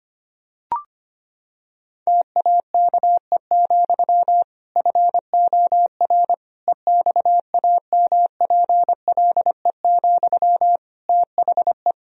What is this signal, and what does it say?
Morse code, heard as a keyed tone, about 25 wpm, 700 Hz. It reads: TAKE, FOR EXAMPLE, THE
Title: Sentences from Top 1000 Words - Review Part 1 - 25wpm